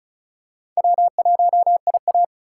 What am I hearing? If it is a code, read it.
Morse code, heard as a keyed tone, about 34 wpm, 700 Hz. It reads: W1IA
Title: USA Call Signs - 4 Characters - 2x Character Spacing - 35wpm